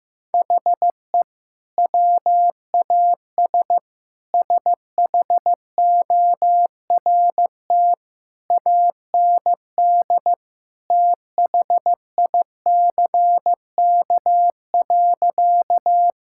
Morse code, heard as a keyed tone, about 15 wpm, 700 Hz. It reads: HE WAS SHORT AND THICK.
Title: Sentences from Top 1600 Words 15wpm